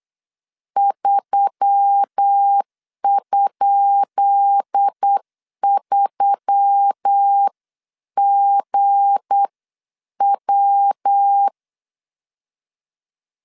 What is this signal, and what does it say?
Morse code, heard as a keyed tone, about 8 wpm, spaced as for 7 wpm, 800 Hz. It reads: 3?3GW
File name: LEZIONI / 10 SIGNS / 10B INT 8wpm 3534